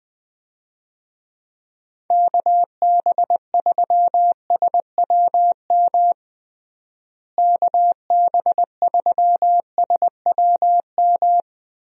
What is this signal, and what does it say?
Morse code, heard as a keyed tone, about 20 wpm, 700 Hz. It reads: KB3SWM KB3SWM
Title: USA Call Signs - 6 Characters - Repeated 3x - 20wpm